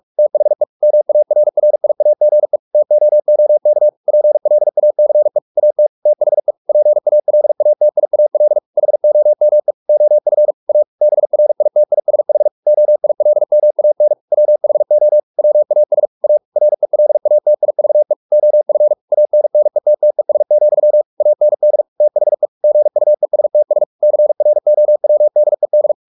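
Morse code, heard as a keyed tone, about 44 wpm, 600 Hz. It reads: THE MARRIAGE TOOK PLACE AT THE PALATIAL HOME OF A BRITISH OILMAN WHO WAS A RELATIVE OF ANNETTES, AND THE GUESTS CROWDED
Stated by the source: Morse practice at 45 wpm, the novel Never by Ken Follett in English, from CW med Gnister.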